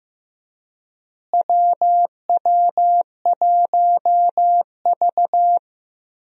Morse code, heard as a keyed tone, about 15 wpm, 700 Hz. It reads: WW1V
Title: USA Call Signs - 4 Characters - Repeated 2x - 15wpm